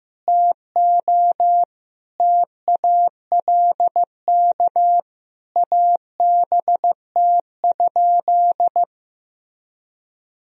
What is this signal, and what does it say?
Morse code, heard as a keyed tone, about 15 wpm, 700 Hz. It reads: TO TALK ABT?
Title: Sentences from Top 400 Words and Abbreviations 15wpm